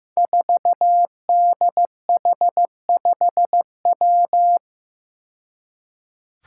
Morse code, heard as a keyed tone, about 15 wpm, 700 Hz. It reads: 4DH5W
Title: Five Challenging Letter-Number Combinations 15wpm